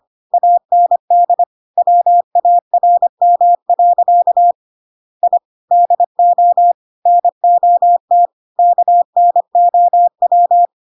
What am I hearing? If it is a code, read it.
Morse code, heard as a keyed tone, about 25 wpm, 700 Hz. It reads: AND WARM. I DO NOT KNOW
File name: Pinocchio / Pinocchio ch1 25 25 ik1hge